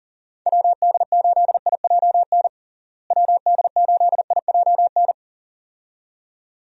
Morse code, heard as a keyed tone, about 40 wpm, 700 Hz. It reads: WD8IJN WD8IJN
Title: USA Call Signs - 6 Characters - Repeated 2x - 40wpm